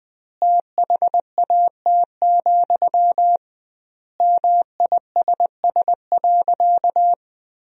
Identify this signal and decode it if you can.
Morse code, heard as a keyed tone, about 20 wpm, 700 Hz. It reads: THAT, MISS.
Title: 2x Word-Spacing - Sentences from Top 500 Words - 20wpm